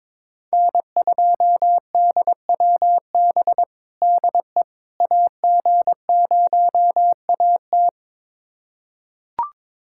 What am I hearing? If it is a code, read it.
Morse code, heard as a keyed tone, about 22 wpm, 700 Hz. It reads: N2DWB DE AG0AT
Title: USA Callsign DE Callsign - 5 Characters - 22wpm